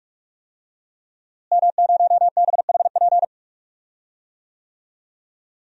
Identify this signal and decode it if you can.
Morse code, heard as a keyed tone, about 46 wpm, 700 Hz. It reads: M0BHP
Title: DXCC-100 Call Signs - 5 Characters - Repeated 3x - 45wpm